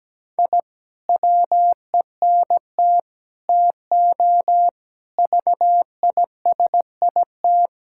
Morse code, heard as a keyed tone, about 17 wpm, 700 Hz. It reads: I WENT TO VISIT
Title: Sentences from Top 1000 Words - Review Part 6 - 17wpm